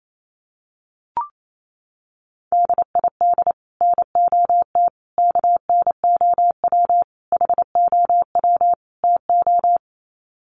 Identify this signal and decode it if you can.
Morse code, heard as a keyed tone, about 28 wpm, 700 Hz. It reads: DID NOT KNOW HOW TO
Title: Sets of 5 Words 28wpm